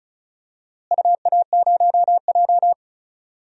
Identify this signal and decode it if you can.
Morse code, heard as a keyed tone, about 34 wpm, 700 Hz. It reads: UA0J
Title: DXCC-100 Call Signs - 4 Characters - Repeated 3x - 35wpm